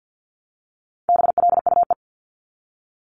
Morse code, heard as a keyed tone, about 50 wpm, 700 Hz. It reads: BLUE
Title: Multi-letter Phonemes Repeated 3x 50wpm